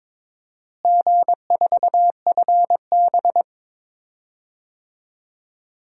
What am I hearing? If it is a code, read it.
Morse code, heard as a keyed tone, about 22 wpm, 700 Hz. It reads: G4FB
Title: DXCC-100 Call Signs - 4 Characters - Repeated 2x - 22wpm